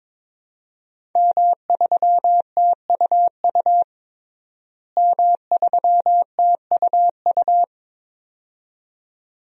Morse code, heard as a keyed tone, about 22 wpm, 700 Hz. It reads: M3TUU M3TUU
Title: DXCC-100 Call Signs - 5 Characters - Repeated 2x - 22wpm